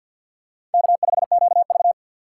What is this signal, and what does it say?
Morse code, heard as a keyed tone, about 53 wpm, 700 Hz. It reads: K5QV
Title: USA Call Signs - 6x Character Spacing - 50wpm